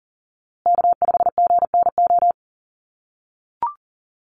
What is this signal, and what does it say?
Morse code, heard as a keyed tone, about 40 wpm, 700 Hz. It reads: K5GNO
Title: USA Call Signs Repeated 2x 40wpm